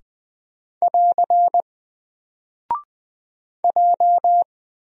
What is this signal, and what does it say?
Morse code, heard as a keyed tone, about 20 wpm, 700 Hz. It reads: <AR> J
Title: Single Character (Letter:Number:Punctuation:Prosign) 20wpm